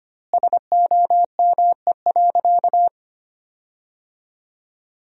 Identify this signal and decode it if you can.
Morse code, heard as a keyed tone, about 25 wpm, 700 Hz. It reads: SOME.
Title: 2x Word-Spacing - Sentences from Top 100 Words - 25wpm